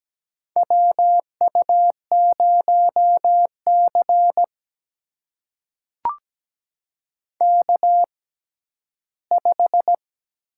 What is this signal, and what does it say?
Morse code, heard as a keyed tone, about 17 wpm, spaced as for 11 wpm, 700 Hz. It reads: WU0C K5
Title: USA Call Signs - 4 Characters - 6x Character Spacing - 17wpm